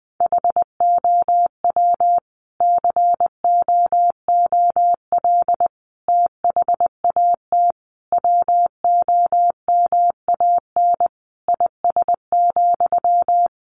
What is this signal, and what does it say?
Morse code, heard as a keyed tone, about 20 wpm, 700 Hz. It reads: HOW COOL THAT WOMAN IS,
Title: Sentences from Top 1200 Words 20wpm